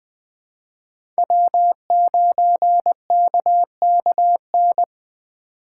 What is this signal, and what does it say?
Morse code, heard as a keyed tone, about 20 wpm, 700 Hz. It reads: W9KKN
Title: USA Call Signs - 4x Character Spacing - 20wpm